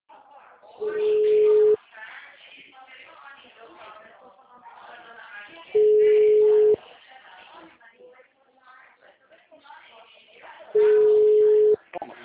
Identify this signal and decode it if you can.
Ringback tone (Europe, 425 Hz), with speech.
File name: out- -2011- - - .63777